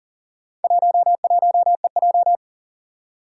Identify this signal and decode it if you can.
Morse code, heard as a keyed tone, about 40 wpm, 700 Hz. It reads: 11EJ
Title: Four Challenging Letter-Number Combinations 40wpm